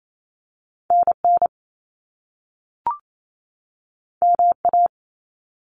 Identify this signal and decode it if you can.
Morse code, heard as a keyed tone, about 28 wpm, 700 Hz. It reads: NN MA
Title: Top Two Letter Combinations in Words 28wpm